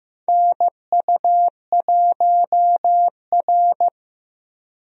Morse code, heard as a keyed tone, about 15 wpm, 700 Hz. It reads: NU1R
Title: USA Call Signs - 4 Characters - Repeated 2x - 15wpm